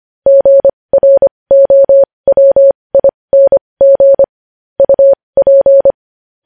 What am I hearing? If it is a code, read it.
Morse code, heard as a keyed tone, about 25 wpm, spaced as for 20 wpm, 550 Hz. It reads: GROWING UP
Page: news-headlines-in-morse-code-at-20-wpm